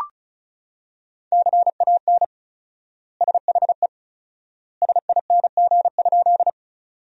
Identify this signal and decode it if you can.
Morse code, heard as a keyed tone, about 36 wpm, 700 Hz. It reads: CAN SHE SING?
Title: 4x Word-Spacing - Sentences from Top 400 Words - 35wpm